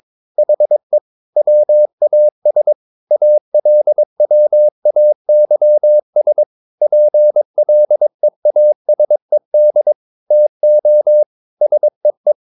Morse code, heard as a keyed tone, about 22 wpm, 600 Hz. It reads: HE WAS ALWAYS PLEASED TO SEE
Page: Romanen Never af Ken Follet på engelsk.